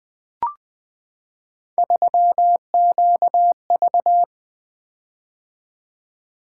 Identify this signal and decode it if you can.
Morse code, heard as a keyed tone, about 20 wpm, 700 Hz. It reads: 3QV
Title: Three Letter-Number Combinations - 20wpm